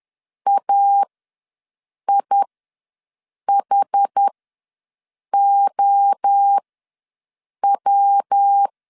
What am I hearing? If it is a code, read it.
Morse code, heard as a keyed tone, about 10 wpm, spaced as for 6 wpm, 800 Hz. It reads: AIHOW
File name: LEZIONI / 5 LBVC / 5 LBV ws 10wpm 3645